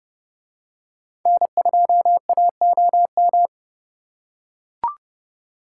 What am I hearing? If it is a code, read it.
Morse code, heard as a keyed tone, about 29 wpm, 700 Hz. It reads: N2AOM